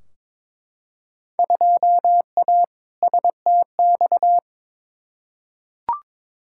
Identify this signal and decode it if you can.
Morse code, heard as a keyed tone, about 22 wpm, 700 Hz. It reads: 2A STX
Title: Field Day Exchanges - 22wpm